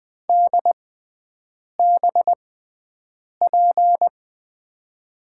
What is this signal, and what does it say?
Morse code, heard as a keyed tone, about 20 wpm, spaced as for 7 wpm, 700 Hz. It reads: DBP
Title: USA Call Signs - 6 Characters - 6x Character Spacing - 20wpm